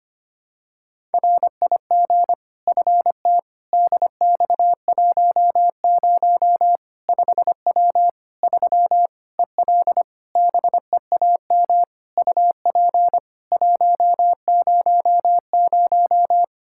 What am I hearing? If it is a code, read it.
Morse code, heard as a keyed tone, about 25 wpm, 700 Hz. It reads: RIG FT DX10 5W 3 EL BEAM UP 100
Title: Practice QSO - Exchange 2 - Info - 25wpm